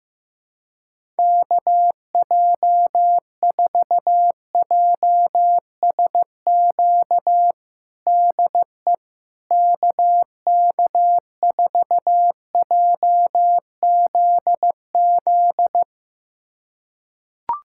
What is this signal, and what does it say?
Morse code, heard as a keyed tone, about 16 wpm, 700 Hz. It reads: KJ4JSQ DE KK4JZZ
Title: USA Callsign DE Callsign 15wpm